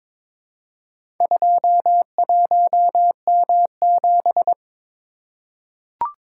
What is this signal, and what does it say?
Morse code, heard as a keyed tone, about 22 wpm, 700 Hz. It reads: 21M7
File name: practice / 22wpm / Four Letter-Number Combinations - 22wpm